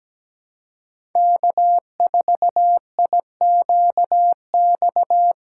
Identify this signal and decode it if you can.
Morse code, heard as a keyed tone, about 17 wpm, 700 Hz. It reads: K4IQX
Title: USA Call Signs - 5 Characters - 4x Character Spacing - 17wpm